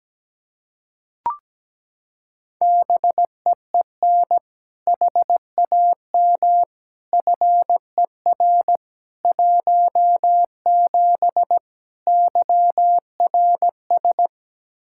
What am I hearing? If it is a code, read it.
Morse code, heard as a keyed tone, about 17 wpm, 700 Hz. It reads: BEEN HAM FER 17 YRS
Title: Practice QSO - Exchange 3 - More Info - 17wpm